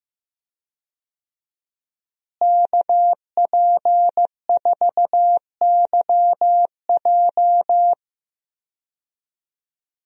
Morse code, heard as a keyed tone, about 15 wpm, 700 Hz. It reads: KP4YJ
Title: DXCC-100 Call Signs - 5 Characters - Repeated 2x - 15wpm